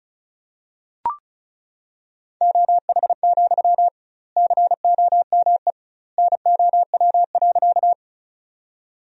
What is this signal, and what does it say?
Morse code, heard as a keyed tone, about 36 wpm, 700 Hz. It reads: OH, COME NOW.
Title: 2x Word-Spacing - Sentences from Top 500 Words - 35wpm